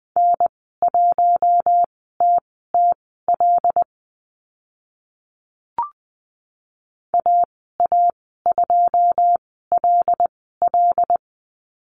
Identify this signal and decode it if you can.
Morse code, heard as a keyed tone, about 20 wpm, spaced as for 14 wpm, 700 Hz. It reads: N1TTL AA2LL
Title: 082 Lesson 14g - Number 2 - Call Signs 20wpm